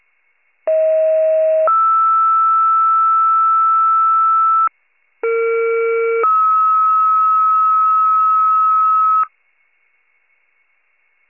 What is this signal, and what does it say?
Two-tone page, heard as a one-second tone, then a three-second tone, about 640 then 1320 Hz, about 460 then 1290 Hz. It is heard from 0.7 to 9.3 s.